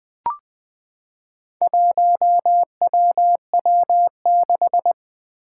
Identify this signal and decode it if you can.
Morse code, heard as a keyed tone, about 20 wpm, 700 Hz. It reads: 1WW6
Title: Four Challenging Letter-Number Combinations 20wpm